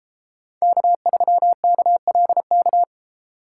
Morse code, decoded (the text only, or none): K3KLK